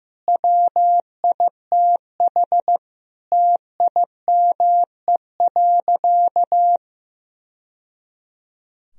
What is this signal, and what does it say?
Morse code, heard as a keyed tone, about 16 wpm, 700 Hz. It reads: WITH TIME.
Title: Sentences from Top 1000 Words - Review Part 3 - 15wpm